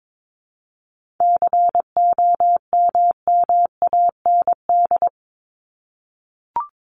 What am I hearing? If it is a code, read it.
Morse code, heard as a keyed tone, about 22 wpm, 700 Hz. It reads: COMMAND
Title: Top 1501-1750 Words 22wpm